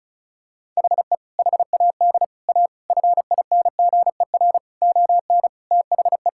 Morse code, heard as a keyed tone, about 36 wpm, 700 Hz. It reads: HE HAD A FINGER ON THE
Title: Sentences from Top 1600 Words 35wpm